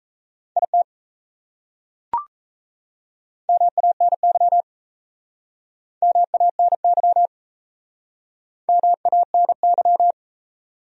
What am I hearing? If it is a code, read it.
Morse code, heard as a keyed tone, about 37 wpm, 700 Hz. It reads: IT MANY MANY MANY